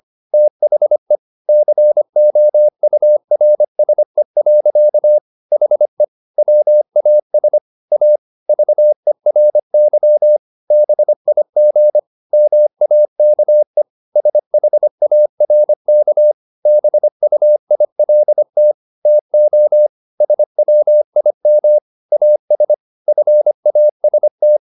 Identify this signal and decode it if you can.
Morse code, heard as a keyed tone, about 25 wpm, 600 Hz. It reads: THE COURSE. HE WAS A VERY BIG MAKE SHARK BUILT TO SWIM AS FAST